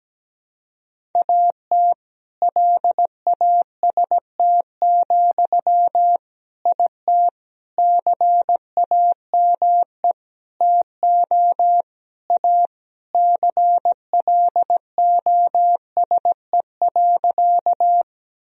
Morse code, heard as a keyed tone, about 17 wpm, 700 Hz. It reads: AT LAST, IT CAME TO A CLOSE.
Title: Sentences from Top 1000 Words - Review Part 1 - 17wpm